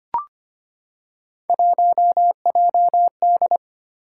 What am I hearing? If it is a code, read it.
Morse code, heard as a keyed tone, about 25 wpm, 700 Hz. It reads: 1JD
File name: Three Letter-Number Combinations - 25wpm